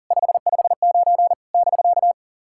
Morse code, heard as a keyed tone, about 40 wpm, 700 Hz. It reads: 559 <BK>
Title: SOTA QSO Practice 40wpm